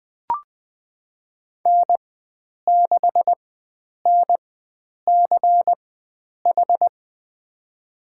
Morse code, heard as a keyed tone, about 20 wpm, spaced as for 9 wpm, 700 Hz. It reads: N6NCH
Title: USA Call Signs - 4x Character Spacing - 20wpm